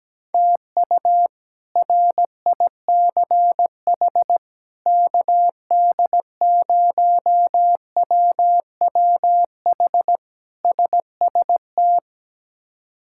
Morse code, heard as a keyed tone, about 17 wpm, 700 Hz. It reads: TU RICH KD0WWH SST
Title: K1USN SST QSO Practice - 17wpm